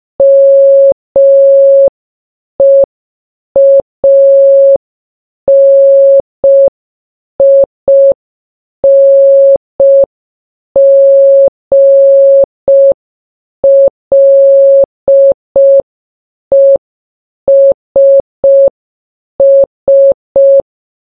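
Morse code, heard as a keyed tone, about 5 wpm, 550 Hz. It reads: MEANINGLESS
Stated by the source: Morse sound from the game Baba Is You.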